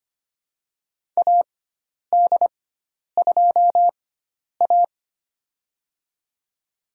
Morse code, heard as a keyed tone, about 25 wpm, spaced as for 10 wpm, 700 Hz. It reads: AD2A
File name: Lesson 16g - Letter c - Call Signs 25wpm